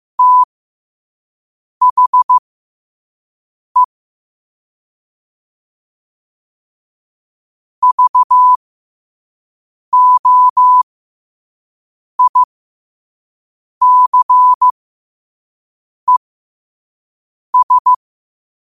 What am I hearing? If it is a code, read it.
Morse code, heard as a keyed tone, about 15 wpm, spaced as for 5 wpm, 1000 Hz. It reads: THE VOICES
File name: qotdfiles / 5wpm